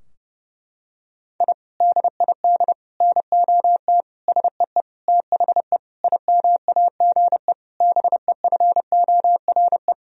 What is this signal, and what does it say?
Morse code, heard as a keyed tone, about 30 wpm, 700 Hz. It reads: I DID NOT SEE THE IMAGE BEFORE
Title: Sentences from Top 1000 Words - Review Part 4 - 30wpm